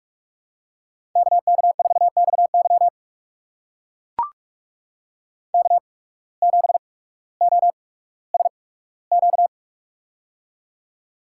Morse code, heard as a keyed tone, about 42 wpm, spaced as for 19 wpm, 700 Hz. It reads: KK4XY K7OSQ
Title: USA Call Signs - 5 Characters - 8x Character Spacing - 45wpm